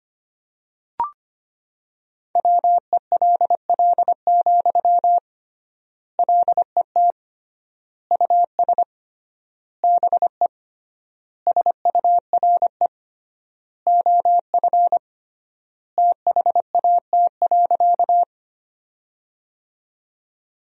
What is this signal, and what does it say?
Morse code, heard as a keyed tone, about 25 wpm, 700 Hz. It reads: WELL, LET US BE SURE OF THAT.
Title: 3x Word-Spacing - Sentences from Top 300 Words - 25wpm